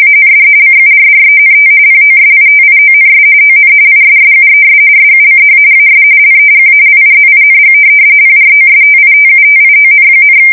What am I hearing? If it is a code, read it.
RTTY, about 45.45 baud. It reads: AND CURRENT POSITIONS ON ISSUES SUCH AS LEGALIZING PROSTITUTION,
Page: 170 Shift 2125/2295